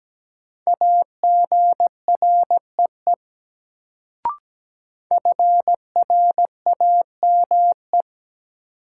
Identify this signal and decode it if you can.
Morse code, heard as a keyed tone, about 17 wpm, 700 Hz. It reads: AGREE FRAME
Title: Sentences from Top 1000 Words - Review Part 6 - New Word Drill - 17wpm